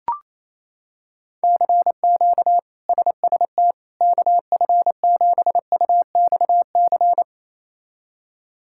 Morse code, heard as a keyed tone, about 28 wpm, 700 Hz. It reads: CQ SST KF7UXC
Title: K1USN SST QSO Practice - 28wpm